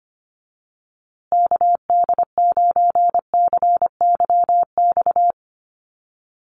Morse code, heard as a keyed tone, about 25 wpm, 700 Hz. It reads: KD9CYX
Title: USA Call Signs Repeated 3x 25wpm